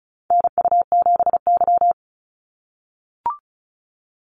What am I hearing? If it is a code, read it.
Morse code, heard as a keyed tone, about 34 wpm, 700 Hz. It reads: NU7Y
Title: Call Signs - 35wpm